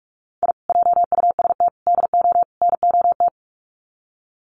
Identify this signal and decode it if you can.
Morse code, heard as a keyed tone, about 44 wpm, 700 Hz. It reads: I JUST DO NOT
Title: Sets of 4 Words 45wpm